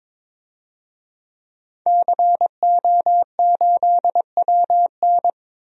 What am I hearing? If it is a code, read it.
Morse code, heard as a keyed tone, about 21 wpm, 700 Hz. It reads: CO8WN
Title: DXCC-100 Call Signs - 5 Characters - Repeated 2x - 22wpm